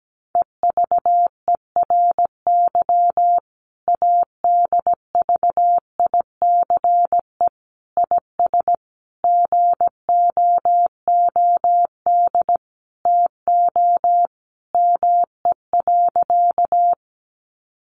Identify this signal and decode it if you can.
Morse code, heard as a keyed tone, about 17 wpm, 700 Hz. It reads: EVERY ADVICE IS GOOD TO ME.
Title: Sentences from Top 1100 Words 17wpm